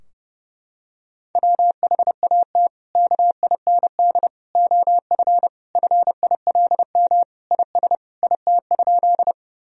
Morse code, heard as a keyed tone, about 32 wpm, 700 Hz. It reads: WHAT KIND OF FILM IS IT?
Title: Sentences from Top 1000 Words - Review Part 3 - 30wpm